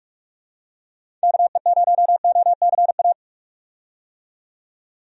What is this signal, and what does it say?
Morse code, heard as a keyed tone, about 44 wpm, 700 Hz. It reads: KE0OCA
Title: USA Call Signs - 6 Characters - 45wpm